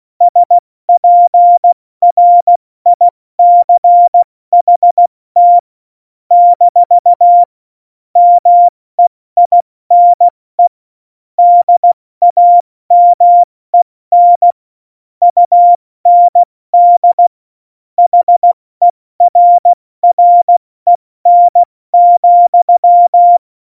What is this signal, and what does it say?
Morse code, heard as a keyed tone, about 16 wpm, spaced as for 14 wpm, 700 Hz. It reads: SPRICHT - MEINE DAMEN UND HERREN,